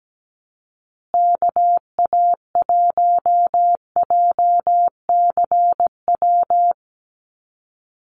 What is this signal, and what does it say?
Morse code, heard as a keyed tone, about 17 wpm, 700 Hz. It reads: KA1JCW
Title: USA Call Signs Repeated 2x 17wpm